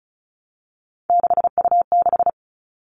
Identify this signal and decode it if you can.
Morse code, heard as a keyed tone, about 34 wpm, 700 Hz. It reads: 6U6
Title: All Cut Numbers - 3-Digit 35wpm